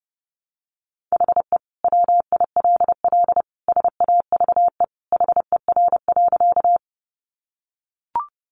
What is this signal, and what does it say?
Morse code, heard as a keyed tone, about 30 wpm, 700 Hz. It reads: HE WILL SAVE HER.
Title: Sentences from Top 1100 Words 30wpm